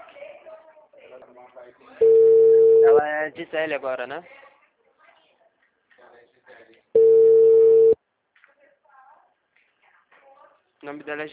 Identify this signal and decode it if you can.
Ringback tone (China, 450 Hz), with speech.
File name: out- -2017